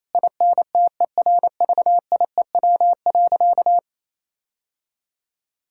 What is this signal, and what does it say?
Morse code, heard as a keyed tone, about 28 wpm, 700 Hz. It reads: INTERVIEW.
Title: Sentences from Top 1600 Words 28wpm